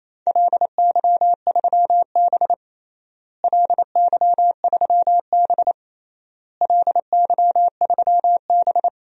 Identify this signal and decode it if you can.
Morse code, heard as a keyed tone, about 27 wpm, 700 Hz. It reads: LY3B LY3B LY3B
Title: DXCC-100 Call Signs - 4 Characters - Repeated 3x - 28wpm